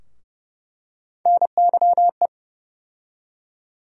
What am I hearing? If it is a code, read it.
Morse code, heard as a keyed tone, about 30 wpm, 700 Hz. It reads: NYE
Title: USA Call Sign Postfixes - 3 Letters 30wpm